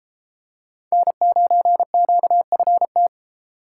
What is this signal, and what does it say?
Morse code, heard as a keyed tone, about 31 wpm, 700 Hz. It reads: N9QFT